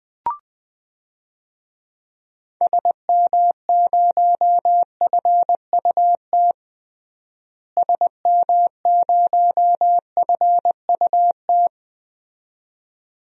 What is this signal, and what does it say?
Morse code, heard as a keyed tone, about 20 wpm, 700 Hz. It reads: SM0FUT SM0FUT
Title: DXCC-100 Call Signs - 6 Characters - Repeated 2x - 20wpm